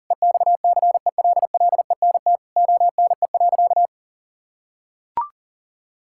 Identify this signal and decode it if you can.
Morse code, heard as a keyed tone, about 40 wpm, 700 Hz. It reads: EXCELLENT ONE.
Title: Sentences from Top 1000 Words - Review Part 6 - 40wpm